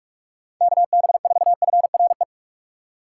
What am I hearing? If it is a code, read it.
Morse code, heard as a keyed tone, about 41 wpm, 700 Hz. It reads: KB4FRE
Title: USA Call Signs - 6 Characters - 2x Character Spacing - 45wpm